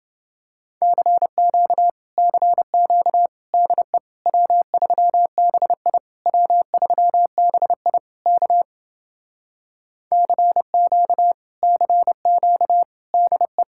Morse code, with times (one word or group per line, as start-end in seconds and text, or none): 0.8-1.9 CQ
2.2-3.3 CQ
3.5-4.0 DE
4.3-6.0 W3BI
6.3-8.0 W3BI
8.3-8.6 K
10.1-11.3 CQ
11.6-12.8 CQ
13.1-13.6 DE